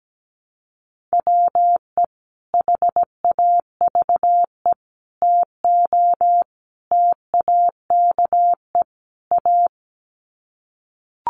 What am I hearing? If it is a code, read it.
Morse code, heard as a keyed tone, about 17 wpm, 700 Hz. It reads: WE HAVE TO TAKE A
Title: Sets of 5 Words Encore 17wpm